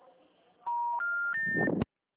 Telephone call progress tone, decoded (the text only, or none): special information tone